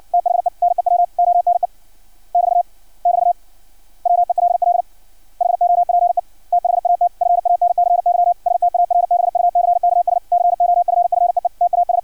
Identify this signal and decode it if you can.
Morse code, heard as a keyed tone, about 56 wpm, 700 Hz. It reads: THE NEW ONE <BT> <BT> WELL HOPE THAT FANCY STARBUCKS COFFEE TAS
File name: archived / CRSnet